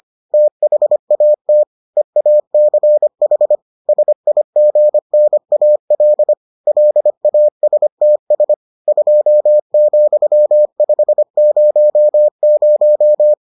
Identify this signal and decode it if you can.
Morse code, heard as a keyed tone, about 25 wpm, 600 Hz. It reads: THAT EACH SIGNAL LASTS 2,500